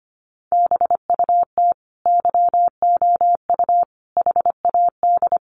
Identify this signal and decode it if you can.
Morse code, heard as a keyed tone, about 25 wpm, 700 Hz. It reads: BUT YOU HAD
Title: Sentences from Top 1100 Words 25wpm